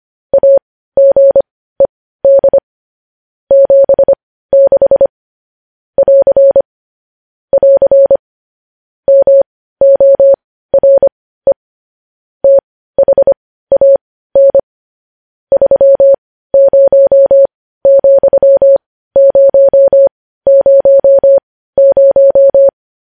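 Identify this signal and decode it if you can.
Morse code, heard as a keyed tone, about 25 wpm, spaced as for 15 wpm, 550 Hz. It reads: AGED 76 <AR> <AR> MORE THAN 30,000
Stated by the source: Morse code news headlines at 15.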